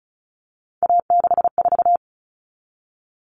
Morse code, heard as a keyed tone, about 34 wpm, 700 Hz. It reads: A64
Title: All Cut Numbers - 3-Digit 35wpm